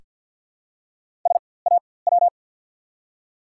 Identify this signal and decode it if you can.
Morse code, heard as a keyed tone, about 48 wpm, spaced as for 23 wpm, 700 Hz. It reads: SAW
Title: 4x Character Spacing - Top 300 Words - 50wpm